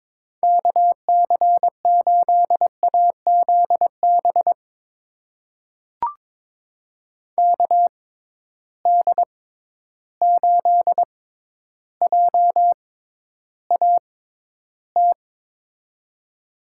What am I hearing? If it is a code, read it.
Morse code, heard as a keyed tone, about 22 wpm, spaced as for 11 wpm, 700 Hz. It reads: KC8AZB KD8JAT
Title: USA Call Signs - 6 Characters - 6x Character Spacing - 22wpm